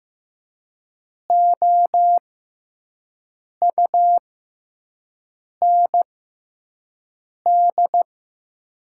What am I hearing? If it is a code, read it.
Morse code, heard as a keyed tone, about 15 wpm, spaced as for 5 wpm, 700 Hz. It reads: OUND
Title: 6x Character Spacing - Top 500 Words - 15wpm